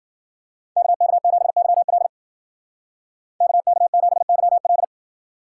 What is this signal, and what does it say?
Morse code, heard as a keyed tone, about 50 wpm, 700 Hz. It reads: KK7YL KK7YL
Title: Speed Racing - USA 2x2 Call Signs 40wpm